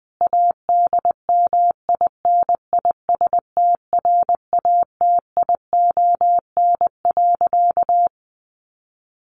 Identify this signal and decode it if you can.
Morse code, heard as a keyed tone, about 20 wpm, 700 Hz. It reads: ADMINISTRATION.
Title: Sentences from Top 1500 Words 20wpm